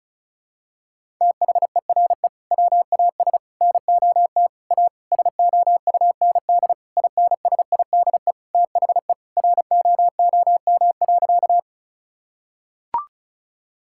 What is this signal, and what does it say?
Morse code, heard as a keyed tone, about 34 wpm, 700 Hz. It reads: THERE WAS NOT A SOUND INSIDE THE ROOM.
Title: Sentences from Top 1000 Words - Review Part 6 - 35wpm